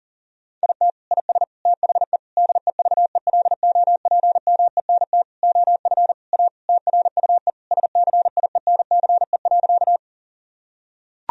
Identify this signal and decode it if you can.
Morse code, heard as a keyed tone, about 40 wpm, 700 Hz. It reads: IT IS THE DEVELOPMENT OF A TRUE SCIENCE.
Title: Sentences from Top 1000 Words - Review Part 4 - 40wpm